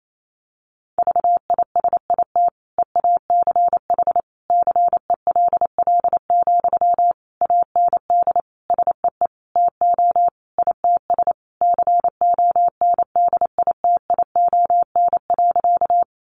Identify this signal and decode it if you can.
Morse code, heard as a keyed tone, about 28 wpm, 700 Hz. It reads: VISIT EACH CELL, AND SEE TO ITS CONDITION.
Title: Sentences from Top 1100 Words 28wpm